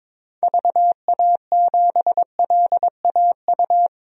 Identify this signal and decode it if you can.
Morse code, heard as a keyed tone, about 22 wpm, 700 Hz. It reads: VA7LAU
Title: DXCC-100 Call Signs - 6 Characters - 22wpm